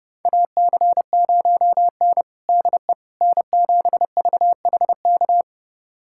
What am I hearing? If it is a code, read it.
Morse code, heard as a keyed tone, about 32 wpm, 700 Hz. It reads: AC0N DE N7VHK
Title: USA Callsign DE Callsign 30wpm